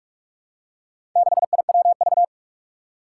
Morse code, heard as a keyed tone, about 44 wpm, 700 Hz. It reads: 6IWV
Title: Four Challenging Letter-Number Combinations 45wpm